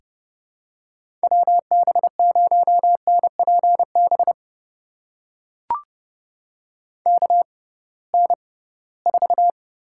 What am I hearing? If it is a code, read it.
Morse code, heard as a keyed tone, about 30 wpm, spaced as for 20 wpm, 700 Hz. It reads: WB0NPB KN4
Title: USA Call Signs - 6 Characters - 6x Character Spacing - 30wpm